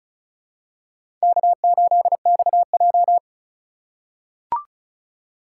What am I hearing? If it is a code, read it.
Morse code, heard as a keyed tone, about 34 wpm, 700 Hz. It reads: K8XJ
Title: K1USN SST QSO Practice - 35wpm